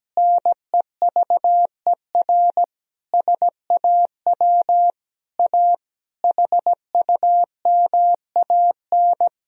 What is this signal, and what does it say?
Morse code, heard as a keyed tone, about 17 wpm, 700 Hz. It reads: NEVER SAW A HUMAN
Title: Sentences from Top 1000 Words - Review Part 3 - 17wpm